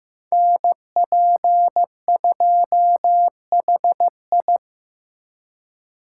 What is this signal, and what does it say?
Morse code, heard as a keyed tone, about 18 wpm, 700 Hz. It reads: NP2HI
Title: Speed Racing - USA 2x2 Call Signs 15wpm